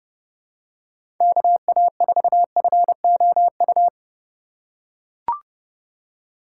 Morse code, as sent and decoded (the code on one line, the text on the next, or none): -.- .- ....- ..-. --- ..-
KA4FOU